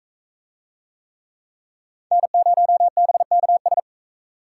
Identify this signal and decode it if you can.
Morse code, heard as a keyed tone, about 39 wpm, 700 Hz. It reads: N0BKS